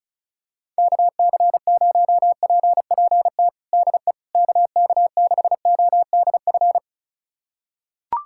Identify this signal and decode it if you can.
Morse code, heard as a keyed tone, about 34 wpm, 700 Hz. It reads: KC0PPT DE KK6ODF
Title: USA Callsign DE Callsign 35wpm